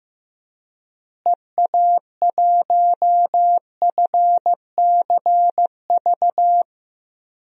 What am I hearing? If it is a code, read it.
Morse code, heard as a keyed tone, about 15 wpm, 700 Hz. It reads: EA1FCV
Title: DXCC-100 Call Signs - 6 Characters - Repeated 2x - 15wpm